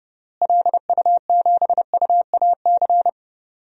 Morse code, heard as a keyed tone, about 30 wpm, 700 Hz. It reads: LU7UAC
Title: DXCC-100 Call Signs - 6 Characters - Repeated 3x - 30wpm